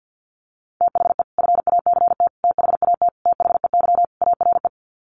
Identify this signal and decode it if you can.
Morse code, heard as a keyed tone, about 49 wpm, 700 Hz. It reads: THE FACT THAT THEY ARE
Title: Sets of 5 Words Encore2 50wpm